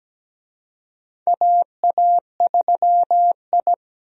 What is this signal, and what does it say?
Morse code, heard as a keyed tone, about 17 wpm, 700 Hz. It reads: AA3I
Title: USA Call Signs - 4 Characters - 17wpm